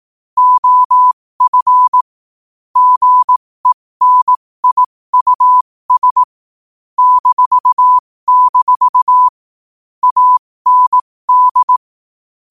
Morse code, heard as a keyed tone, about 18 wpm, spaced as for 15 wpm, 1000 Hz. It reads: OF GENIUS -- AND
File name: qotdfiles / 15wpm